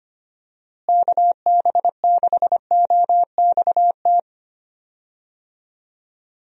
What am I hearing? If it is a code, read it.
Morse code, heard as a keyed tone, about 25 wpm, 700 Hz. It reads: KB6OXT
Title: USA Call Signs - 6 Characters - Repeated 2x - 25wpm